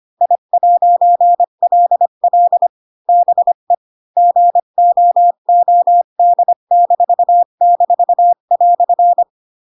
Morse code, heard as a keyed tone, about 25 wpm, 700 Hz. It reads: I'LL BE GOOD--"
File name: Pinocchio / Pinocchio ch8 20 25 ik1hge